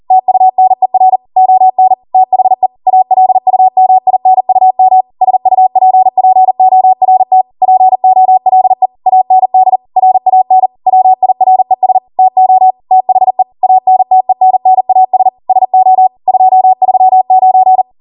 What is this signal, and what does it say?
Morse code, heard as a keyed tone, about 40 wpm, 750 Hz. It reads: TUNER ON THE ALUMINUM SUPPORT POLE AND RAN WIRES TO THE ANTENNAS SO 239